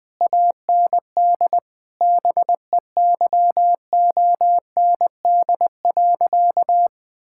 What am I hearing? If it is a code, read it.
Morse code, heard as a keyed tone, about 20 wpm, 700 Hz. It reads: AND BEYOND.
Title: Difficult Sentences with Hints - 20wpm